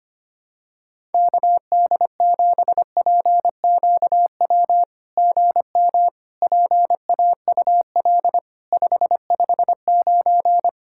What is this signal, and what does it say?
Morse code, heard as a keyed tone, about 25 wpm, 700 Hz. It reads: KD7PQW GM PAUL 559
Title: SOTA QSO Practice 25wpm